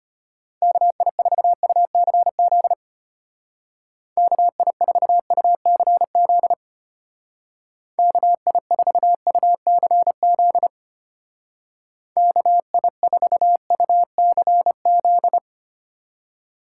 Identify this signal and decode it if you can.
Morse code, heard as a keyed tone, about 31 wpm, 700 Hz. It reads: KI4UCZ KI4UCZ KI4UCZ KI4UCZ